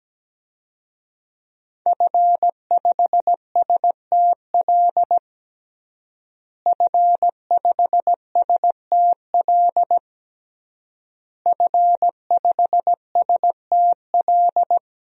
Morse code, heard as a keyed tone, about 17 wpm, 700 Hz. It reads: F5STL F5STL F5STL
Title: DXCC-100 Call Signs - 5 Characters - Repeated 3x - 17wpm